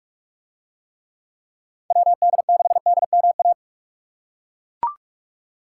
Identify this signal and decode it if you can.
Morse code, heard as a keyed tone, about 42 wpm, 700 Hz. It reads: WD6DMA